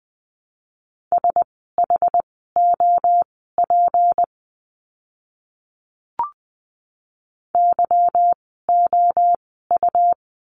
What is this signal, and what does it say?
Morse code, heard as a keyed tone, about 20 wpm, spaced as for 15 wpm, 700 Hz. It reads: SHOP YOU
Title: 153 Lesson 25f - Letter p - Words 20wpm